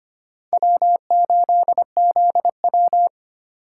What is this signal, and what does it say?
Morse code, heard as a keyed tone, about 32 wpm, 700 Hz. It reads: W8ZW